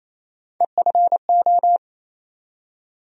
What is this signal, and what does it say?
Morse code, heard as a keyed tone, about 28 wpm, 700 Hz. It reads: EFO
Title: USA Call Sign Postfixes - 3 Letters 28wpm